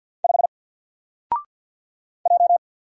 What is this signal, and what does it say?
Morse code, heard as a keyed tone, about 50 wpm, 700 Hz. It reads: 5 J
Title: Single Letter-Number Alt 50wpm